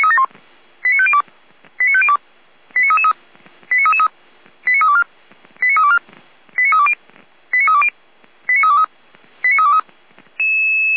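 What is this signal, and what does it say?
Five-tone selective call (ZVEI), heard as quick runs of five tones, tones of about 70 ms. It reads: 89502 89502 89333 89324 89324 89320 89320 89323 89323
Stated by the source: ZVEI five-tone call received by SDR, recorded by Feuerwehr Orlamünde.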